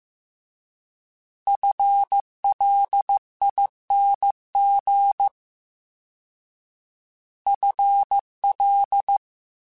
Morse code, heard as a keyed tone, about 15 wpm, 800 Hz. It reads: FLING FL
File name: Combinations / Combinations-mp3s / Combinations@13